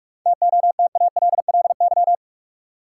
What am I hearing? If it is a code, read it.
Morse code, heard as a keyed tone, about 46 wpm, 700 Hz. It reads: TOTALLY
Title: Sentences from Top 1600 Words - New Word Drill Alt - 45wpm